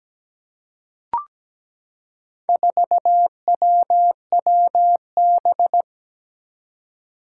Morse code, heard as a keyed tone, about 17 wpm, 700 Hz. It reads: 4WWB
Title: Four Challenging Letter-Number Combinations 17wpm